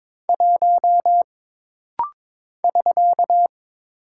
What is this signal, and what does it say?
Morse code, heard as a keyed tone, about 22 wpm, 700 Hz. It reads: 1 <SK>
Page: This rapid-fire timing is geared to push your ICR (Instant Character Recognition) skills.